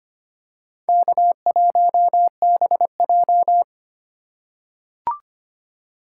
Morse code, heard as a keyed tone, about 25 wpm, 700 Hz. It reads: K1BJ
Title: USA Call Signs - 4 Characters - 2x Character Spacing - 25wpm